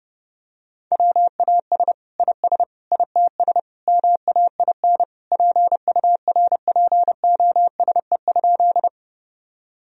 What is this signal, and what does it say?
Morse code, heard as a keyed tone, about 30 wpm, 700 Hz. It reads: WAS IS ITS MAIN PURPOSE?
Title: Sentences from Top 1000 Words - Review Part 2 - 30wpm